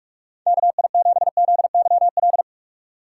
Morse code, heard as a keyed tone, about 47 wpm, 700 Hz. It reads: KI7ZYL
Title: USA Call Signs - 6 Characters - 4x Character Spacing - 45wpm